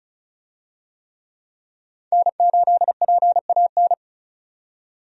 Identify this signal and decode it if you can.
Morse code, heard as a keyed tone, about 36 wpm, 700 Hz. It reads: N8PAN
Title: USA Call Signs - 5 Characters - 35wpm